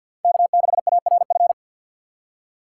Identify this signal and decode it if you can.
Morse code, heard as a keyed tone, about 48 wpm, 700 Hz. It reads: K6ARF
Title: USA Call Signs - 3x Character Spacing - 50wpm